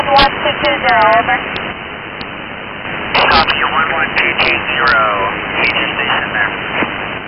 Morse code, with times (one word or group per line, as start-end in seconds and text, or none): none